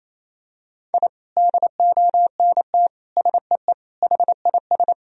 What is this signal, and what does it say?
Morse code, heard as a keyed tone, about 28 wpm, 700 Hz. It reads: I DONT SEE HIS
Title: Sentences from Top 1000 Words - Review Part 1 - 28wpm